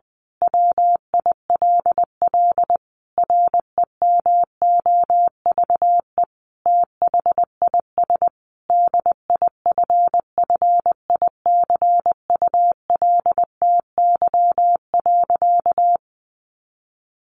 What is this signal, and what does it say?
Morse code, heard as a keyed tone, about 20 wpm, 700 Hz. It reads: WILL REMOVE THIS DIFFICULTY.
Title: Sentences from Top 1500 Words 20wpm